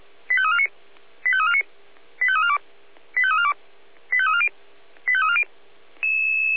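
Five-tone selective call (ZVEI), heard as quick runs of five tones, tones of about 70 ms. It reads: 85349 85349 85342 85342 85340 85340
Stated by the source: ZVEI five-tone selective call received on SDR.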